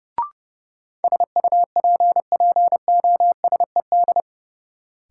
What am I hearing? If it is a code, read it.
Morse code, heard as a keyed tone, about 31 wpm, 700 Hz. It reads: SUPPOSED
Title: Sets of 3 Words - New Word Drill - 30wpm